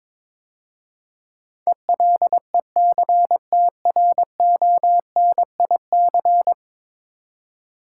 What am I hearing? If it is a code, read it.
Morse code, heard as a keyed tone, about 22 wpm, 700 Hz. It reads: ELECTRONIC
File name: practice / 22wpm / 10 Letter Words - First 3 Letters Given - 22wpm